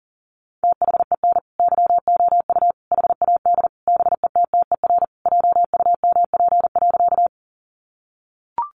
Morse code, heard as a keyed tone, about 40 wpm, 700 Hz. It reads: THEN YOU HAD BETTER JUMP.
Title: Sentences from Top 1300 Words 40wpm